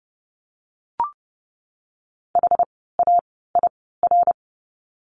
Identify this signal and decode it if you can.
Morse code, heard as a keyed tone, about 30 wpm, spaced as for 17 wpm, 700 Hz. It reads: HAIR
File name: Lesson 17f - Letter u - Words 30wpm